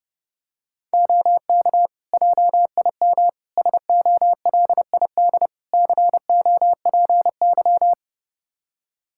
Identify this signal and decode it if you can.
Morse code, heard as a keyed tone, about 30 wpm, 700 Hz. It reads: OK JIM SOLID COPY
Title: Practice QSO - Exchange 3 - More Info - 30wpm